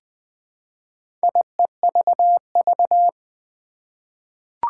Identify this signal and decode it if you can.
Morse code, heard as a keyed tone, about 20 wpm, 700 Hz. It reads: IEVV
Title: Four Challenging Letter-Number Combinations 20wpm